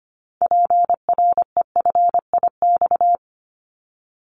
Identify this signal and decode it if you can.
Morse code, heard as a keyed tone, about 25 wpm, 700 Hz. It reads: PREFIX
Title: Callsign Prefixes for Top 25 DXCC - 25wpm